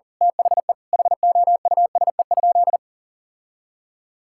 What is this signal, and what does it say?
Morse code, heard as a keyed tone, about 40 wpm, 700 Hz. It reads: THE HOUSE?
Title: Sentences from Top 1000 Words - Review Part 5 - 40wpm